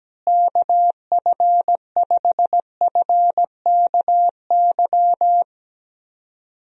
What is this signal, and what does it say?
Morse code, heard as a keyed tone, about 17 wpm, 700 Hz. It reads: KF5FKY
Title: USA Call Signs - 6 Characters - Repeated 2x - 17wpm